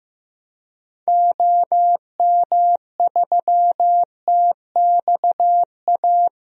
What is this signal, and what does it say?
Morse code, heard as a keyed tone, about 15 wpm, 700 Hz. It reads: OM3TXA
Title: DXCC-100 Call Signs - 6 Characters - Repeated 2x - 15wpm